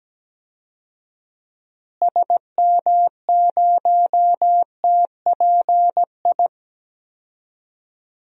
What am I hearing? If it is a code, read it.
Morse code, heard as a keyed tone, about 17 wpm, 700 Hz. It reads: SM0TPI
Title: DXCC-100 Call Signs - 6 Characters - Repeated 2x - 17wpm